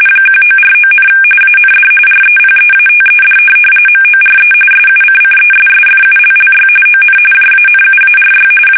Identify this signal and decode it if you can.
RTTY, about 45.45 baud. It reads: 1949, AND REPEATED MORE SUCCESSFULLY ON JANUARY
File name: fsk850 / teletype-RTTY